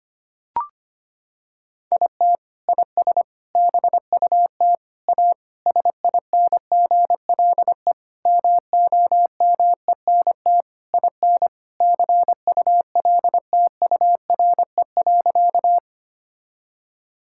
Morse code, heard as a keyed tone, about 25 wpm, 700 Hz. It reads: IT IS BUT A SINGLE MOMENT IN CULTURE.
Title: Sentences from Top 1000 Words - Review Part 6 - 25wpm